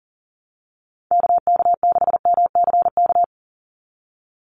Morse code, heard as a keyed tone, about 39 wpm, 700 Hz. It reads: KK6MCK
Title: USA Call Signs 40wpm